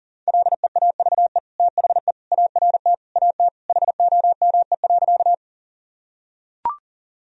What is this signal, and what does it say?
Morse code, heard as a keyed tone, about 40 wpm, 700 Hz. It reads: LEAVE THE ART AT HOME.
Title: Sentences from Top 1000 Words - Review Part 3 - 40wpm